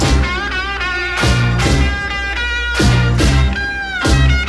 Emergency Alert System (SAME) tones, heard as harsh buzzing bursts. It is heard from 0.3 to 3.9 s.